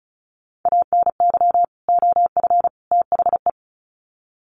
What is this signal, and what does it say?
Morse code, heard as a keyed tone, about 34 wpm, 700 Hz. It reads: ANY OF THE
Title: Sets of 3 Words 35wpm